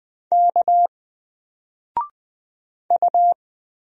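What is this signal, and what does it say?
Morse code, heard as a keyed tone, about 20 wpm, 700 Hz. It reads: K U
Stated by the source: Single letter and number Morse drill at 20 wpm, from N4LCD.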